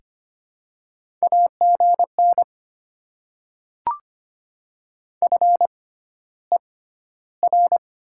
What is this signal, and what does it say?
Morse code, heard as a keyed tone, about 25 wpm, spaced as for 13 wpm, 700 Hz. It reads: AGN FER
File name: Top 20 QSO Elements - 6x Character Spacing - 25wpm